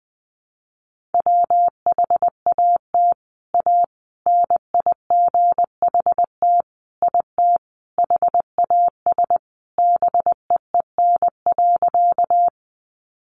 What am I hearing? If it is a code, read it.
Morse code, heard as a keyed tone, about 20 wpm, 700 Hz. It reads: WHAT A NIGHT IT HAS BEEN.
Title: Sentences from Top 300 Words 20wpm